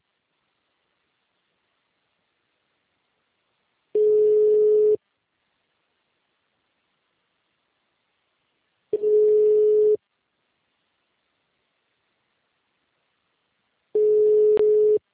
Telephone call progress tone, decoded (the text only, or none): ringback tone